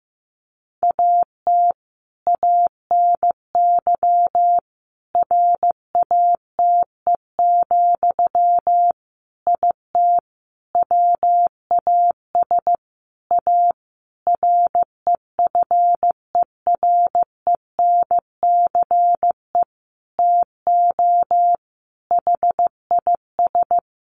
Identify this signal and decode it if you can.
Morse code, heard as a keyed tone, about 15 wpm, 700 Hz. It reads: AT ANY RATE, IT WAS A REFERENCE TO HIS
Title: Sentences from Top 1200 Words 15wpm